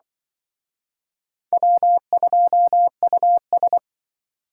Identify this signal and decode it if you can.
Morse code, heard as a keyed tone, about 23 wpm, 700 Hz. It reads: W2US
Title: Speed Racing - USA 1x2 Call Signs 20wpm